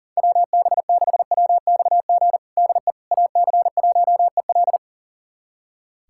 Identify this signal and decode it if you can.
Morse code, heard as a keyed tone, about 40 wpm, 700 Hz. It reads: WB6WXG DE AC1EL
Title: USA Callsign DE Callsign 40wpm